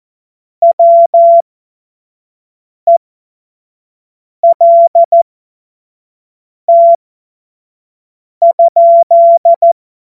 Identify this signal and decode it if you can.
Morse code, heard as a keyed tone, about 14 wpm, spaced as for 5 wpm, 680 Hz. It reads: WELT?
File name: Morsekurs / Witze / Witz 30. August 2022 25 F70